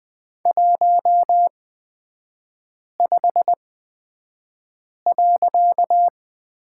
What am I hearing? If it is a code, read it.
Morse code, heard as a keyed tone, about 20 wpm, spaced as for 5 wpm, 700 Hz. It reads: 15.
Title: 255 Lesson 41e - Period - Three Characters 20wpm